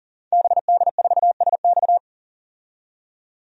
Morse code, heard as a keyed tone, about 40 wpm, 700 Hz. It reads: BD4SX
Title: DXCC-100 Call Signs - 5 Characters - 40wpm